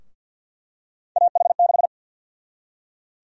Morse code, heard as a keyed tone, about 51 wpm, 700 Hz. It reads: AH6
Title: USA Call Sign Prefixes - Repeated 2x - 50wpm